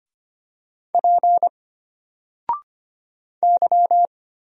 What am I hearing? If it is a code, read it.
Morse code, heard as a keyed tone, about 25 wpm, 700 Hz. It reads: P Y
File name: Single-Letter-Number-25wpm